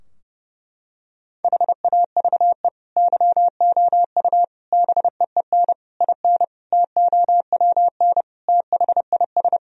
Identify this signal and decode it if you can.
Morse code, heard as a keyed tone, about 30 wpm, 700 Hz. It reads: HAVE YOU BEEN IN TOWN THIS
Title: Sentences from Top 1000 Words - Review Part 2 - 30wpm